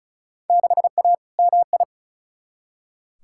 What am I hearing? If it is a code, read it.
Morse code, heard as a keyed tone, about 36 wpm, 700 Hz. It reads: 6A MI
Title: Field Day Exchanges - 35wpm